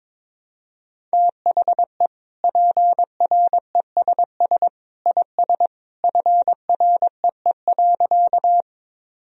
Morse code, heard as a keyed tone, about 21 wpm, 700 Hz. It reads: THE PRESS IS FREE.
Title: Sentences from Top 1000 Words - Review Part 1 - 22wpm